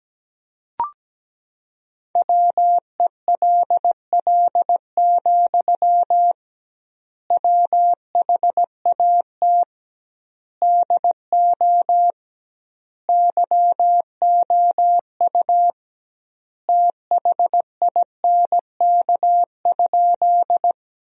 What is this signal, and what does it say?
Morse code, heard as a keyed tone, about 17 wpm, 700 Hz. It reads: WELL, WHAT DO YOU THINK?
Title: 2x Word-Spacing - Sentences from Top 200 Words - 17wpm